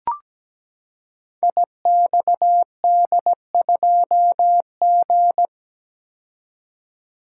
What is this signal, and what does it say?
Morse code, heard as a keyed tone, about 17 wpm, 700 Hz. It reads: IXD2G
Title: Code Groups - 5 Letter:Numbers - 17wpm